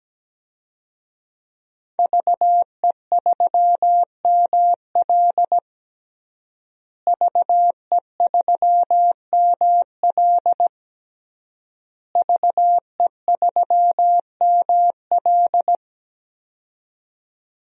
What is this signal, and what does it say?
Morse code, heard as a keyed tone, about 17 wpm, 700 Hz. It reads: VE3ML VE3ML VE3ML
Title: DXCC-100 Call Signs - 5 Characters - Repeated 3x - 17wpm